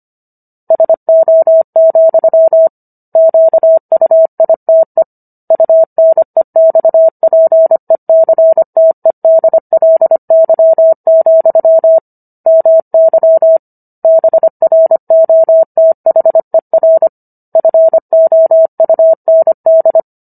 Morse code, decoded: SO, QUITE UNEXPECTEDLY, MY BROTHER FOUND

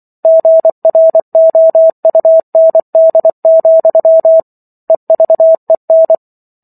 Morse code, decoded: GROUND, EVEN

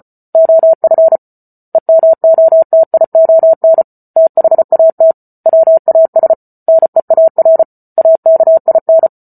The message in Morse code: OF EMOTION THAT WAS NEAR AKIN

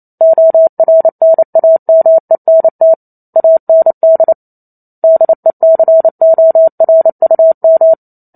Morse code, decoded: ORNAMENT AND DECORUM

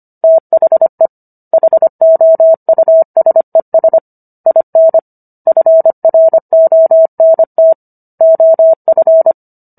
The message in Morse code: THE HOUSES IN FRONT OF

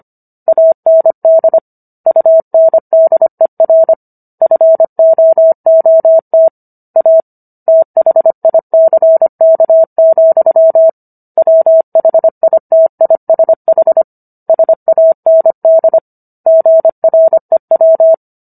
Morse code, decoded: AND UNDER FOOT A THICK, WHITISH SAND GREW

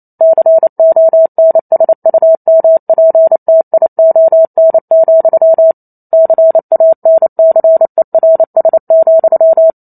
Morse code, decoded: CONSUMPTION, CANCERS,

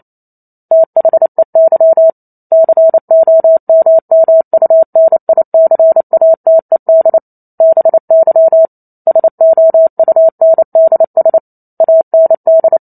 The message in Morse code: THEY COMMUNICATED BY SOUNDS AND

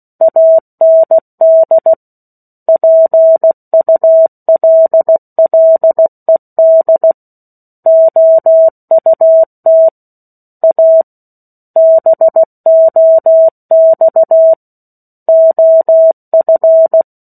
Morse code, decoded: AND PULLED OUT A BOX OF

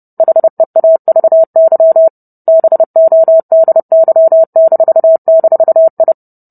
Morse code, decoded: HEAVY BODY--I

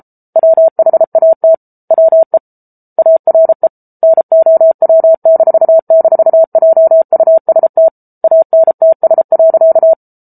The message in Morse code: WHAT WE ARE NOW--JUST ANTS.

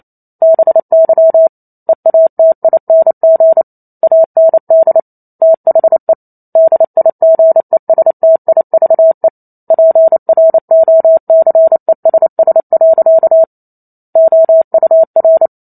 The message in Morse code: BY EATING AND THE DIGESTIVE PROCESS. OUR

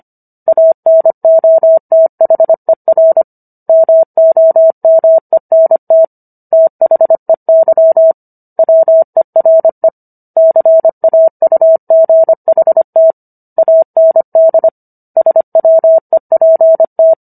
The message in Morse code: ANOTHER MOMENT THEY WERE CAUGHT AND SWEPT

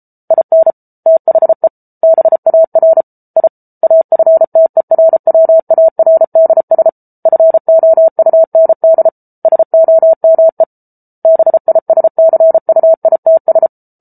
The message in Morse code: IN THE BAR I AFTERWARDS FOUND SOME BISCUITS

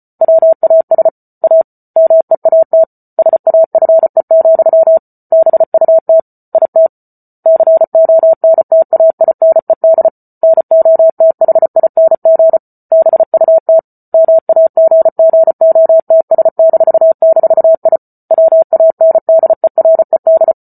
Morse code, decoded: WAS A MEAT SAFE, BUT IT CONTAINED NOTHING BUT MAGGOTS--I WANDERED

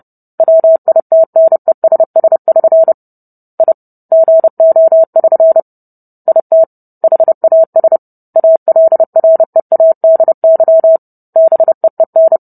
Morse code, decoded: WITNESS<SN> I GO<SN> IT HAS ALREADY BEEN

